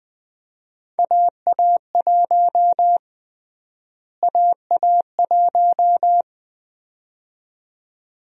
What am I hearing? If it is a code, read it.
Morse code, heard as a keyed tone, about 20 wpm, 700 Hz. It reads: AA1 AA1
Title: USA Call Sign Prefixes - Repeated 2x - 20wpm